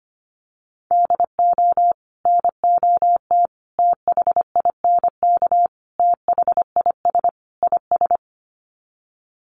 Morse code, heard as a keyed tone, about 25 wpm, 700 Hz. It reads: DO NOT THINK THIS IS
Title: Sets of 5 Words Encore2 25wpm